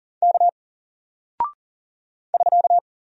Morse code, heard as a keyed tone, about 40 wpm, 700 Hz. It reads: K <SK>
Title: Single Character (Letter:Number:Punctuation:Prosign) Alt 40wpm